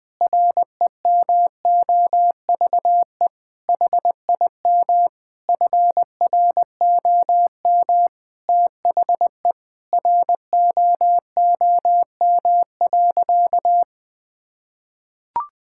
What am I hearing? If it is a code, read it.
Morse code, heard as a keyed tone, about 20 wpm, 700 Hz. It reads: REMOVE HIM FROM THE ROOM.
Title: Sentences from Top 1000 Words - Review Part 5 - 20wpm